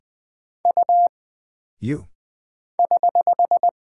Morse code, heard as a keyed tone, about 20 wpm, 700 Hz. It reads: U <HH>
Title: Single Character (Letter:Number:Punctuation:Prosign) - Mind-Melt - 20wpm